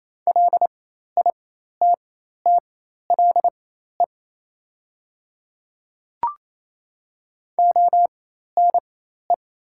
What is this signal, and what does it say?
Morse code, heard as a keyed tone, about 27 wpm, spaced as for 13 wpm, 690 Hz. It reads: LITTLE ONE
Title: 4x Character Spacing - Top 200 Words - 28wpm